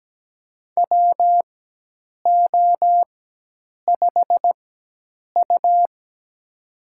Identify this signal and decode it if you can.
Morse code, heard as a keyed tone, about 17 wpm, spaced as for 8 wpm, 700 Hz. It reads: WO5U
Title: USA Call Signs - 4 Characters - 4x Character Spacing - 17wpm